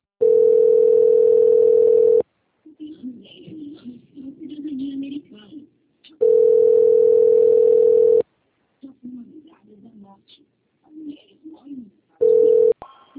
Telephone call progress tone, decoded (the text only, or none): ringback tone